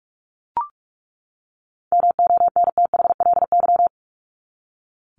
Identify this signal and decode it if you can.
Morse code, heard as a keyed tone, about 44 wpm, 700 Hz. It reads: MONTHLY
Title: LY Suffix - Top 100 Words with Base Words 45wpm